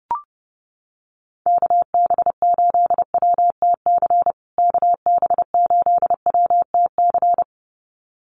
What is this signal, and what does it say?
Morse code, heard as a keyed tone, about 28 wpm, 700 Hz. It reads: KB8WTC KB8WTC
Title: USA Call Signs Repeated 2x 30wpm